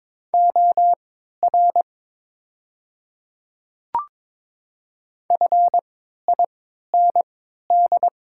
Morse code, heard as a keyed tone, about 22 wpm, spaced as for 13 wpm, 690 Hz. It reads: OR FIND
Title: 3x Character Spacing - Top 500 Words - 22wpm